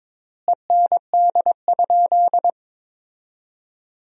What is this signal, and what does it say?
Morse code, heard as a keyed tone, about 22 wpm, 700 Hz. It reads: END?
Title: 4x Word-Spacing - Sentences from Top 200 Words - 22wpm